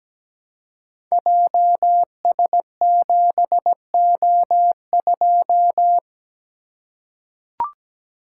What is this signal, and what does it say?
Morse code, heard as a keyed tone, about 17 wpm, 700 Hz. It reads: JS7O2
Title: Code Groups - 5 Letter:Numbers - 17wpm